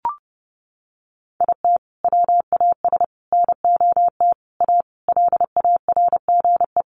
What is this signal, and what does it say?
Morse code, heard as a keyed tone, about 30 wpm, 700 Hz. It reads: IT WAS NOT A LARGE
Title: Sentences from Top 1400 Words 30wpm